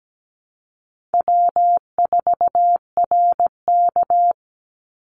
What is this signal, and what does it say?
Morse code, heard as a keyed tone, about 17 wpm, 700 Hz. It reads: W4RK
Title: NA SOTA Chaser Callsigns 17wpm